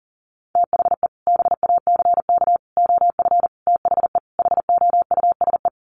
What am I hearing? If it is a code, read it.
Morse code, heard as a keyed tone, about 40 wpm, 700 Hz. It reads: THE BACK OF THE HOUSE
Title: Sets of 5 Words Encore2 40wpm